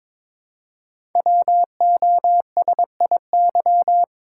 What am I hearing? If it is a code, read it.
Morse code, heard as a keyed tone, about 22 wpm, 700 Hz. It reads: WOSIY
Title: Code Groups - 5 Letter:Numbers - 22wpm